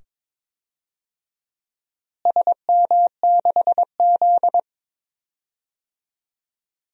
Morse code, heard as a keyed tone, about 22 wpm, 700 Hz. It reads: SM6Z
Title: DXCC-100 Call Signs - 4 Characters - Repeated 2x - 22wpm